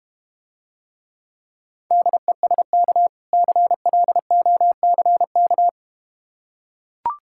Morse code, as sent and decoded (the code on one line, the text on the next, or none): -.. ... -.- / -.-. .-.. --- -.-. -.-
DESK CLOCK